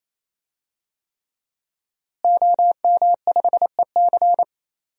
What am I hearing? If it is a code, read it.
Morse code, heard as a keyed tone, about 27 wpm, 700 Hz. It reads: OM5EC
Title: DXCC-100 Call Signs - 5 Characters - Repeated 3x - 28wpm